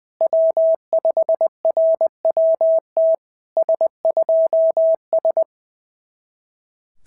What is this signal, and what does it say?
Morse code, heard as a keyed tone, about 17 wpm, 650 Hz. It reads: W5RWT S2S
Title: SOTA QSO Practice 20wpm